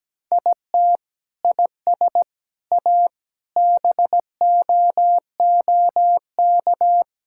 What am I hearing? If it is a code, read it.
Morse code, heard as a keyed tone, about 17 wpm, 700 Hz. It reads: IT IS A BOOK
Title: Sentences from Top 1000 Words - Review Part 5 - 17wpm